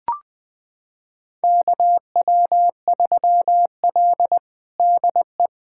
Morse code, heard as a keyed tone, about 21 wpm, 700 Hz. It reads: KW3L DE
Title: USA Callsign DE Callsign - 4 Characters - 20wpm